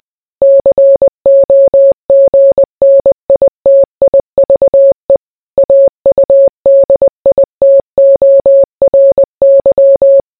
Morse code, heard as a keyed tone, about 20 wpm, 550 Hz. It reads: COGNITIVE AUDITORY